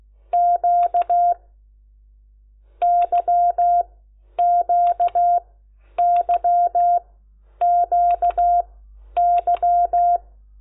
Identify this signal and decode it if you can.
Morse code, heard as a keyed tone, about 16 wpm, 700 Hz. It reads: Q Y Q Y Q Y
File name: Lecciones